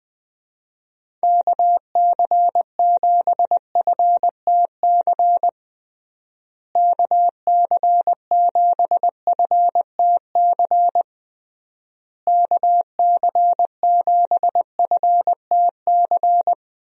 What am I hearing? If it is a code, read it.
Morse code, heard as a keyed tone, about 20 wpm, 700 Hz. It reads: KC7FTC KC7FTC KC7FTC
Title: USA Call Signs - 6 Characters - Repeated 3x - 20wpm